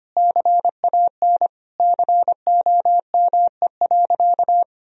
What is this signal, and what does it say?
Morse code, heard as a keyed tone, about 25 wpm, 700 Hz. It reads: CAN COME.
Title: Sentences from Top 1000 Words - Review Part 1 - 25wpm